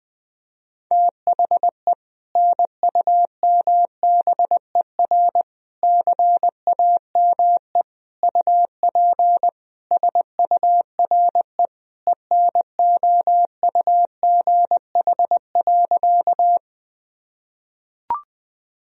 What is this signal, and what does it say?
Morse code, heard as a keyed tone, about 20 wpm, 700 Hz. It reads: THE NUMBER CAME UP SURE ENOUGH.
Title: Sentences from Top 1000 Words - Review Part 1 - 20wpm